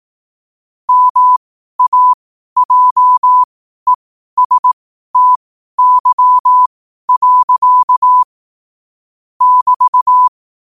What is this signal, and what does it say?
Morse code, heard as a keyed tone, about 18 wpm, spaced as for 12 wpm, 1000 Hz. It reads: MAJESTY. <BT>
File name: qotdfiles / 12wpm